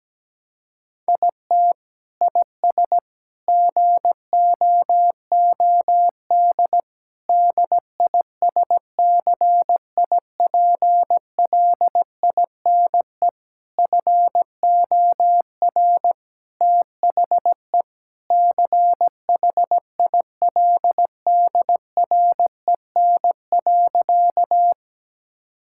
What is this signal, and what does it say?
Morse code, heard as a keyed tone, about 17 wpm, 700 Hz. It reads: IT IS GOOD DISCIPLINE FOR THE CHILDREN.
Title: Sentences from Top 1000 Words - Review Part 6 - 17wpm